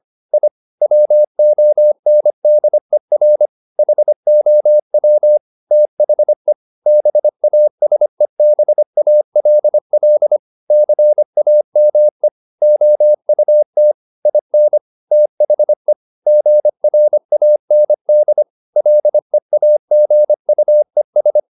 Morse code, heard as a keyed tone, about 25 wpm, 600 Hz. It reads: I WONDER HOW THE BASEBALL CAME OUT IN THE GRAND LEAGUES
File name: the old man and the sea part 1 25wpm